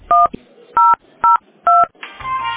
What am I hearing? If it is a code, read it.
DTMF, heard as telephone touch tones, tones of about 180 ms. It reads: 1002